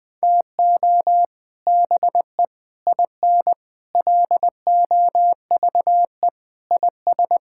Morse code, heard as a keyed tone, about 21 wpm, 700 Hz. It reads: TO BE IN LOVE IS
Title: Sentences with 3-Word Phrases - 20wpm